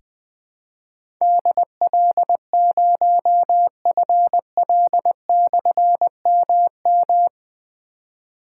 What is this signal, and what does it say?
Morse code, heard as a keyed tone, about 20 wpm, 700 Hz. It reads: DL0FL/MM
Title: Difficult Call Signs 20wpm